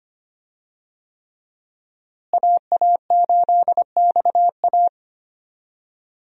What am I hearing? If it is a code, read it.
Morse code, heard as a keyed tone, about 25 wpm, 700 Hz. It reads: AA8XA
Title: USA Call Signs - 5 Characters - Repeated 3x - 25wpm